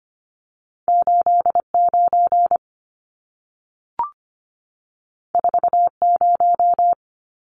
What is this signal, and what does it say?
Morse code, heard as a keyed tone, about 25 wpm, 700 Hz. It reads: 89 40
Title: Numbers - 2-Digit 25wpm